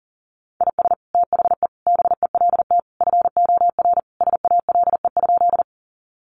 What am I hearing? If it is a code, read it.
Morse code, heard as a keyed tone, about 40 wpm, 700 Hz. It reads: IS THE BELT FOR SALE?
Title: Sentences from Top 1400 Words 40wpm